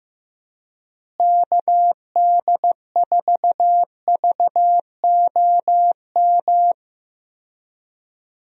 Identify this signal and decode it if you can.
Morse code, heard as a keyed tone, about 15 wpm, 700 Hz. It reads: KD4VOM
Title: USA Call Signs - 6 Characters - Repeated 3x - 15wpm